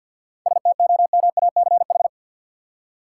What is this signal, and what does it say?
Morse code, heard as a keyed tone, about 50 wpm, 700 Hz. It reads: STOMACH
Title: Seven Letter Words - 50wpm